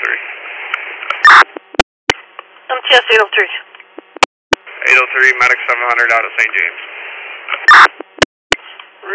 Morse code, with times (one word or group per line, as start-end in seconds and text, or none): none